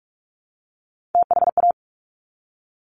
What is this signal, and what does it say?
Morse code, heard as a keyed tone, about 46 wpm, 700 Hz. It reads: THA
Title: Top Three Letter Combinations in Words 45wpm